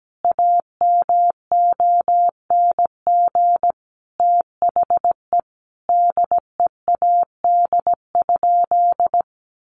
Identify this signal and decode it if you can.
Morse code, heard as a keyed tone, about 17 wpm, 700 Hz. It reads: AMONG THE DEAD?
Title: Sentences from Top 1500 Words 17wpm